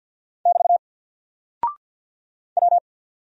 Morse code, heard as a keyed tone, about 48 wpm, 700 Hz. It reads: <BT> W
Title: Single Character (Letter:Number:Punctuation:Prosign) Alt 50wpm